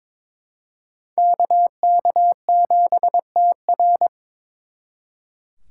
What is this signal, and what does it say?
Morse code, heard as a keyed tone, about 26 wpm, 700 Hz. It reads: KK7TR
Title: Speed Racing - USA 2x2 Call Signs 22wpm